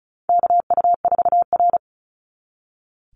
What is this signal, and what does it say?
Morse code, heard as a keyed tone, about 34 wpm, 700 Hz. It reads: KU4R
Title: NA SOTA Chaser Callsigns 35wpm